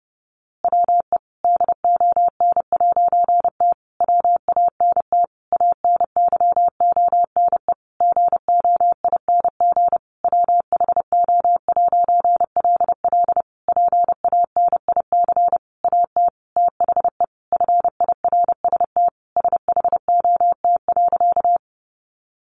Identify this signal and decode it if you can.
Morse code, heard as a keyed tone, about 30 wpm, 700 Hz. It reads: WE DON'T WANT ANYONE GOING WHO'LL PANIC AT THE FIRST SHOT.
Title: A Sound Of Thunder 30wpm